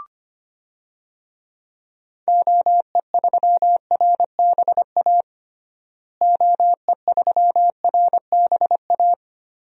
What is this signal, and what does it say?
Morse code, heard as a keyed tone, about 25 wpm, 700 Hz. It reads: OE3RBA OE3RBA
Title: DXCC-100 Call Signs - 6 Characters - Repeated 3x - 25wpm